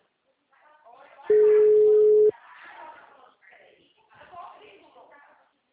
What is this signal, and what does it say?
Ringback tone (Europe, 425 Hz), with speech.